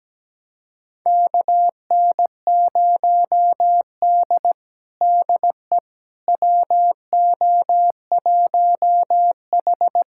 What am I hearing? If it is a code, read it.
Morse code, heard as a keyed tone, about 17 wpm, 700 Hz. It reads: KN0D DE WO1H
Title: USA Callsign DE Callsign - 4 Characters - 17wpm